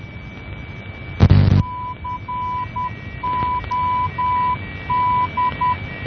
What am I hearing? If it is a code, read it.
Morse code, heard as a keyed tone, about 10 wpm, 1000 Hz, with noise. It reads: COD
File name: COD NDB 1988